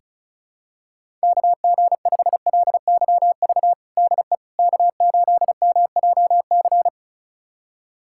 Morse code, heard as a keyed tone, about 36 wpm, 700 Hz. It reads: KG5LYV DE K8MJC
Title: USA Callsign DE Callsign 35wpm